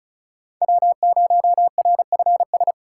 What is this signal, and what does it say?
Morse code, heard as a keyed tone, about 36 wpm, 700 Hz. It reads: W0RFS
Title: USA Call Signs - 5 Characters - 3x Character Spacing - 35wpm